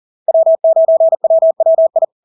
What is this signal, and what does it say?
Morse code, heard as a keyed tone, about 40 wpm, 650 Hz. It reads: W9WWI
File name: LARK-FD-2022-Log-in-CW-40wpm-650-Hz